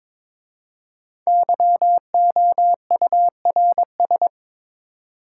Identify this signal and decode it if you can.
Morse code, heard as a keyed tone, about 22 wpm, 700 Hz. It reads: YOURS
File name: 5 Letter Words - First 3 Letters Given - 22wpm